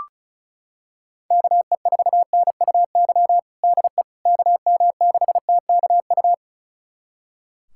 Morse code, heard as a keyed tone, about 34 wpm, 700 Hz. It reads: KE4NUY DE KM6TKU
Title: USA Callsign DE Callsign 35wpm